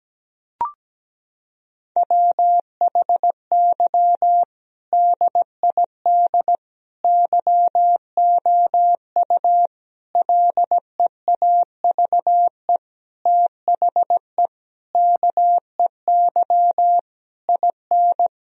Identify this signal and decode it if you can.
Morse code, heard as a keyed tone, about 17 wpm, 700 Hz. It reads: WHY DID YOU LEAVE THE KEY IN
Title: Sentences from Top 1000 Words - Review Part 4 - 17wpm